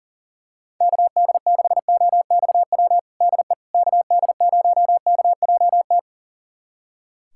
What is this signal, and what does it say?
Morse code, heard as a keyed tone, about 40 wpm, 700 Hz. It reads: KD6OXW DE KD0KJT
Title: USA Callsign DE Callsign - 6 Characters - 40wpm